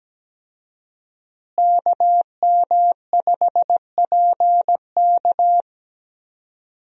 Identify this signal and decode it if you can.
Morse code, heard as a keyed tone, about 17 wpm, 700 Hz. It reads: KM5PK
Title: USA Call Signs - 5 Characters - Repeated 3x - 17wpm